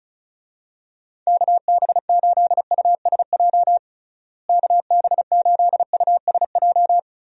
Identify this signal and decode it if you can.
Morse code, heard as a keyed tone, about 36 wpm, 700 Hz. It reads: KB8USJ KB8USJ
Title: USA Call Signs - 6 Characters - Repeated 3x - 35wpm